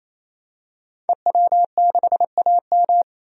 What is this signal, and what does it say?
Morse code, heard as a keyed tone, about 28 wpm, 700 Hz. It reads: EW6AM
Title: DXCC-100 Call Signs - 5 Characters - 28wpm